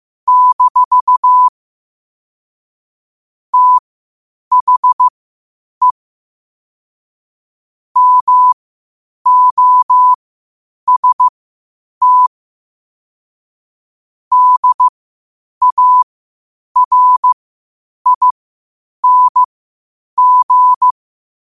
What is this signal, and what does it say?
Morse code, heard as a keyed tone, about 15 wpm, spaced as for 8 wpm, 1000 Hz. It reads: - THE MOST DARING
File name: qotdfiles / 8wpm